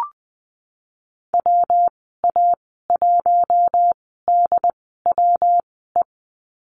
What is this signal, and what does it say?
Morse code, heard as a keyed tone, about 19 wpm, spaced as for 14 wpm, 700 Hz. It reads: WA1DWE
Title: 179 Lesson 29g - Slash - Call Signs 20wpm